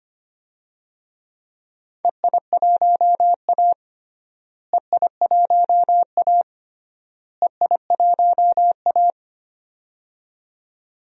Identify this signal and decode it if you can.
Morse code, heard as a keyed tone, about 25 wpm, 700 Hz. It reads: EI1A EI1A EI1A
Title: DXCC-100 Call Signs - 4 Characters - Repeated 3x - 25wpm